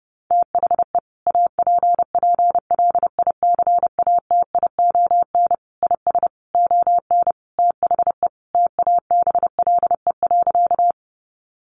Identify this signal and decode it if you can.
Morse code, heard as a keyed tone, about 30 wpm, 700 Hz. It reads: THE APPLICATION IS ON THE TABLE.
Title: Sentences from Top 1200 Words 30wpm